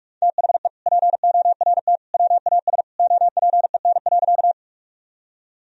Morse code, heard as a keyed tone, about 43 wpm, 700 Hz. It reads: THE PORT WAS OPEN.
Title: Sentences from Top 1000 Words - Review Part 1 - 45wpm